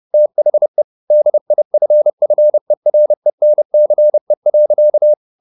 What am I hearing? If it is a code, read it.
Morse code, heard as a keyed tone, about 30 wpm, 600 Hz. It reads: THE DIFFERENCE.
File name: the old man and the sea / the old man and the sea 1 30wpm